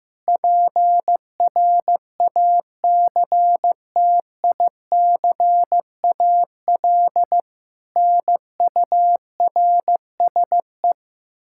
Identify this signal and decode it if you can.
Morse code, heard as a keyed tone, about 15 wpm, 700 Hz. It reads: PRACTICAL NURSE
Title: Practice QSO - Exchange 3 - More Info - 15wpm